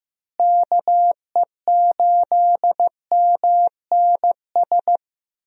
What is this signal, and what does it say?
Morse code, heard as a keyed tone, about 15 wpm, 700 Hz. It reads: KE8MNS
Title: USA Call Signs - 6 Characters - Repeated 3x - 15wpm